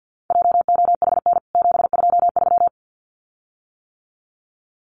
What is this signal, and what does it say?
Morse code, heard as a keyed tone, about 48 wpm, 700 Hz. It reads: JOHN 723
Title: CWT Exchange 50wpm